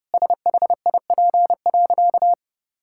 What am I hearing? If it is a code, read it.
Morse code, heard as a keyed tone, about 30 wpm, 700 Hz. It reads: SHIP.
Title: Sentences from Top 1000 Words - Review Part 5 - 30wpm